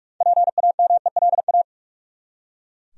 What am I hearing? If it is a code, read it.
Morse code, heard as a keyed tone, about 44 wpm, 700 Hz. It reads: PAMELA
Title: USA Common Names - 45wpm